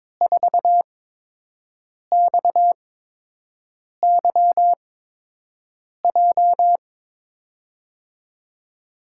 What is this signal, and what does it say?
Morse code, heard as a keyed tone, about 22 wpm, spaced as for 6 wpm, 700 Hz. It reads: 4XYJ
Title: USA Call Signs - 5 Characters - 8x Character Spacing - 22wpm